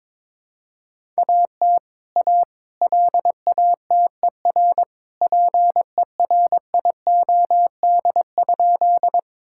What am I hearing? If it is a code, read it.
Morse code, heard as a keyed tone, about 22 wpm, 700 Hz. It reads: AT A LATER PERIOD?
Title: Sentences from Top 1000 Words - Review Part 5 - 22wpm